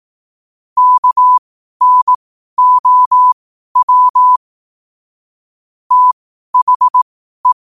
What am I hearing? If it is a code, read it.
Morse code, heard as a keyed tone, about 18 wpm, spaced as for 12 wpm, 1000 Hz. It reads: KNOW THE
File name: qotdfiles / 12wpm